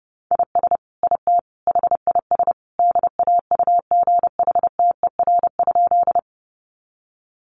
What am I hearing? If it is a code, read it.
Morse code, heard as a keyed tone, about 30 wpm, 700 Hz. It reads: IS IT HIS DAUGHTER?
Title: Sentences from Top 1500 Words 30wpm